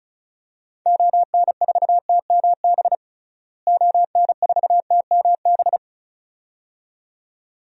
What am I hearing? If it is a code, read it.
Morse code, heard as a keyed tone, about 36 wpm, 700 Hz. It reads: ON4TMB ON4TMB
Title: DXCC-100 Call Signs - 6 Characters - Repeated 2x - 35wpm